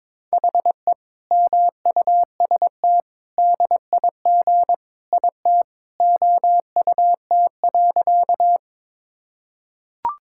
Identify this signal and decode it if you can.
Morse code, heard as a keyed tone, about 22 wpm, 700 Hz. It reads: HE MUST DIG IT OUT.
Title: Sentences from Top 1600 Words 22wpm